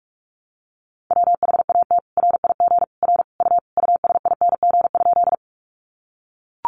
Morse code, heard as a keyed tone, about 44 wpm, 700 Hz. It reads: WHAT RIG R U USING?
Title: Common QSO Phrases 45wpm